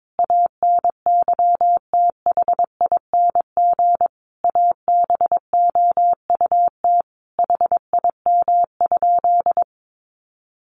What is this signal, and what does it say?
Morse code, heard as a keyed tone, about 22 wpm, 700 Hz. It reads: ANYTHING ABOUT HIM?
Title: Sentences from Top 1000 Words 22wpm